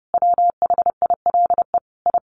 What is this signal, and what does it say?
Morse code, heard as a keyed tone, about 30 wpm, 700 Hz. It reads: WHILE I
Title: Sentences from Top 1500 Words 30wpm